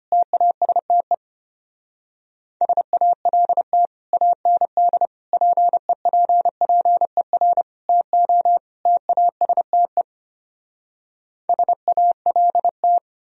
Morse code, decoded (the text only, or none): TASTE SALT AND PEPPER TO TASTE SALT